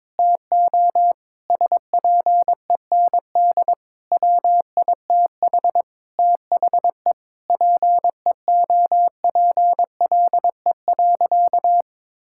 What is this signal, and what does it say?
Morse code, heard as a keyed tone, about 22 wpm, 700 Hz. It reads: TO SPEND WITH THE PEOPLE.
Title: Sentences from Top 1000 Words - Review Part 5 - 22wpm